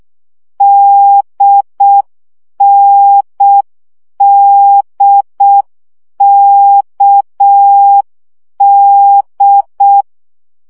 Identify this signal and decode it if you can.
Morse code, heard as a keyed tone, about 6 wpm, 800 Hz. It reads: DNDKD
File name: CW gyakorlas n d k 5 baud